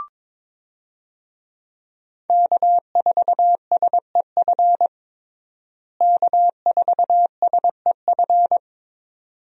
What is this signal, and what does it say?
Morse code, heard as a keyed tone, about 22 wpm, 700 Hz. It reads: K4SEF K4SEF
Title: USA Call Signs - 5 Characters - Repeated 2x - 22wpm